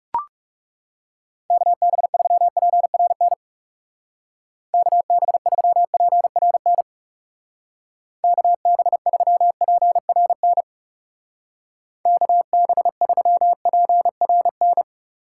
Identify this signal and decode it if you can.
Morse code, heard as a keyed tone, about 35 wpm, 700 Hz. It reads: KB3PRN KB3PRN KB3PRN KB3PRN
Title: Speed Racing - USA 2x3 Call Signs 30wpm